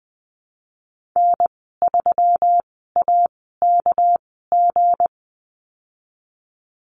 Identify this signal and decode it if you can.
Morse code, heard as a keyed tone, about 20 wpm, spaced as for 14 wpm, 700 Hz. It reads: N3AKG
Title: 238 Lesson 38g - Letter q - Call Signs 20wpm